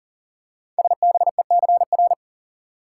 Morse code, heard as a keyed tone, about 38 wpm, 690 Hz. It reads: SBECR
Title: Code Groups - 5 Letters - 40wpm